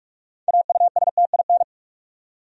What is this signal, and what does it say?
Morse code, heard as a keyed tone, about 46 wpm, 700 Hz. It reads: AUSTIN
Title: USA Common Names - 45wpm